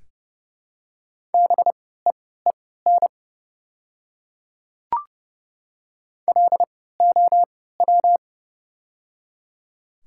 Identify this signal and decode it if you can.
Morse code, heard as a keyed tone, about 31 wpm, spaced as for 17 wpm, 690 Hz. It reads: BEEN LOW
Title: 3x Character Spacing - Top 200 Words - 30wpm